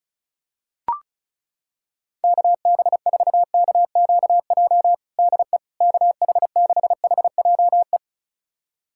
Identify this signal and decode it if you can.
Morse code, heard as a keyed tone, about 36 wpm, 700 Hz. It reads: KB4KQJ DE KH6HJE
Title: USA Callsign DE Callsign 35wpm